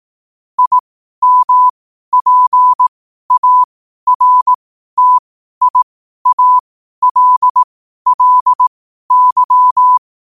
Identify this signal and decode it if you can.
Morse code, heard as a keyed tone, about 18 wpm, spaced as for 12 wpm, 1000 Hz. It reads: IMPARTIALLY
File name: qotdfiles / 12wpm